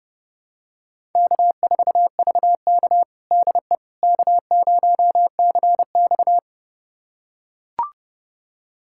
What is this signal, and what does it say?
Morse code, heard as a keyed tone, about 32 wpm, 700 Hz. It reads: K4VK DE K0CX
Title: USA Callsign DE Callsign - 4 Characters - 30wpm